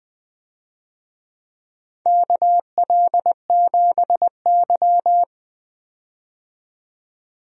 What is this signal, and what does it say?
Morse code, heard as a keyed tone, about 20 wpm, 700 Hz. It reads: KL7Y
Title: DXCC-100 Call Signs - 4 Characters - 20wpm